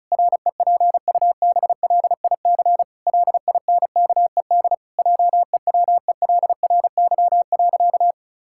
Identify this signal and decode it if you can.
Morse code, heard as a keyed tone, about 36 wpm, 700 Hz. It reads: REPUBLIC LINKED JEWELRY.
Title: Pangram Challenge v2 - 35wpm